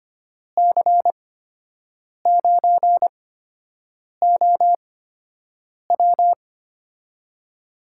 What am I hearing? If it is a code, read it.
Morse code, heard as a keyed tone, about 25 wpm, spaced as for 7 wpm, 700 Hz. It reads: C9OW
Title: USA Call Signs - 5 Characters - 8x Character Spacing - 25wpm